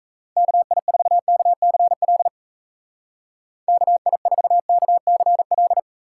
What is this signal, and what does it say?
Morse code, heard as a keyed tone, about 34 wpm, 700 Hz. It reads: KI4KCL KI4KCL
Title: Speed Racing - USA Call Signs 28wpm